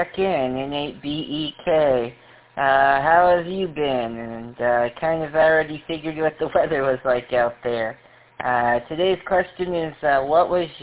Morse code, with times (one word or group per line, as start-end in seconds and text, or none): none